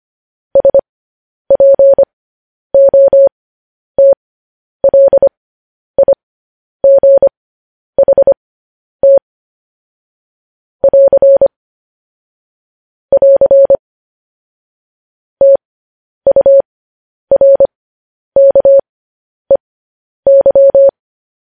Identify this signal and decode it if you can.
Morse code, heard as a keyed tone, about 25 wpm, spaced as for 10 wpm, 550 Hz. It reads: SPOTLIGHT <AR> <AR> TURKEY
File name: podcast / News headlines-1x-25x10